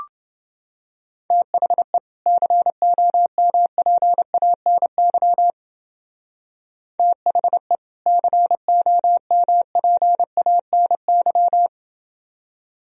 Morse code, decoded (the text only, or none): THE COMPANY THE COMPANY